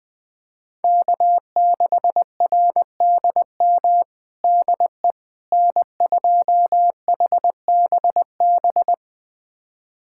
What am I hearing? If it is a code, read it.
Morse code, heard as a keyed tone, about 21 wpm, 700 Hz. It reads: K6RDM DE N2HBB
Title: USA Callsign DE Callsign - 5 Characters - 20wpm